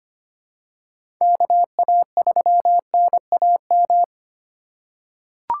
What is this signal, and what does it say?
Morse code, heard as a keyed tone, about 25 wpm, 700 Hz. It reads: KA3NAM
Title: K1USN SST QSO Practice - 25wpm